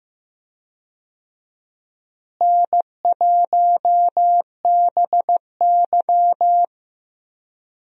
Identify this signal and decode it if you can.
Morse code, heard as a keyed tone, about 15 wpm, 700 Hz. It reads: N1BY
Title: USA Call Signs - 4 Characters - Repeated 2x - 15wpm